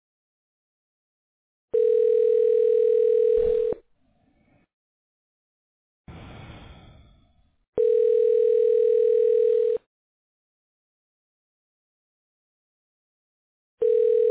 Ringback tone (North America), 440 + 480 Hz.